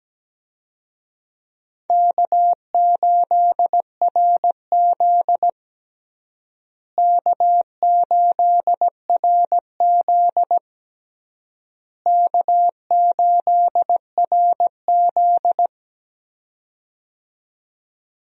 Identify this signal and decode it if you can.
Morse code, heard as a keyed tone, about 17 wpm, 700 Hz. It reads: K8RZ K8RZ K8RZ
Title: USA Call Signs - 4 Characters - Repeated 3x - 17wpm